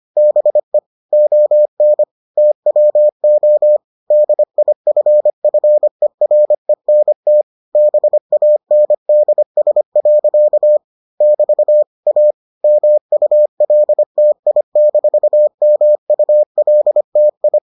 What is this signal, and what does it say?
Morse code, heard as a keyed tone, about 25 wpm, 600 Hz. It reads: BE ON TWO DIFFERENT BANDS. <BT> A MULTI-MULTI